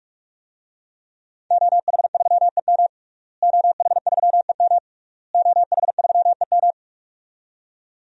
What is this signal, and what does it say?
Morse code, heard as a keyed tone, about 46 wpm, 700 Hz. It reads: OH3EM OH3EM OH3EM
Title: DXCC-100 Call Signs - 5 Characters - Repeated 3x - 45wpm